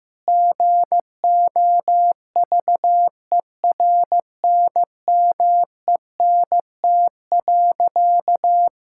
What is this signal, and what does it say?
Morse code, heard as a keyed tone, about 15 wpm, 700 Hz. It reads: GOVERNMENT.
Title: Sentences from Top 1000 Words - Review Part 6 - 15wpm